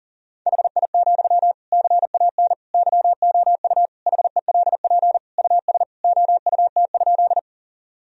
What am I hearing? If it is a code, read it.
Morse code, heard as a keyed tone, about 40 wpm, 700 Hz. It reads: HI, CAN YOU HELP US OUT?
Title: Sentences from Top 1000 Words - Review Part 3 - 40wpm